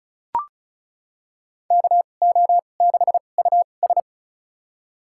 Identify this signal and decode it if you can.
Morse code, heard as a keyed tone, about 36 wpm, spaced as for 25 wpm, 700 Hz. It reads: KO6US
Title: USA Call Signs - 5 Characters - 2x Character Spacing - 35wpm